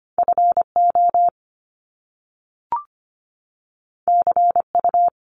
Morse code, heard as a keyed tone, about 25 wpm, 700 Hz. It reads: FO CU
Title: Top Two Letter Combinations in Words 25wpm